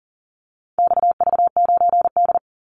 Morse code, heard as a keyed tone, about 40 wpm, 700 Hz. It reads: XV9D
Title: Call Signs Repeated 3x - 40wpm